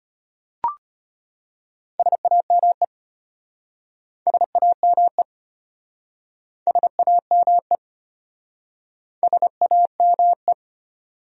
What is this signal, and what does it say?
Morse code, heard as a keyed tone, about 34 wpm, 700 Hz. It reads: SAME SAME SAME SAME